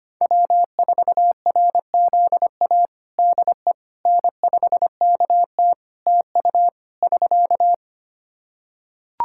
Morse code, heard as a keyed tone, about 25 wpm, 700 Hz. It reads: W4RZA DE N5KT TU <SK>
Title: Practice QSO - Exchange 4 - Ending - 25wpm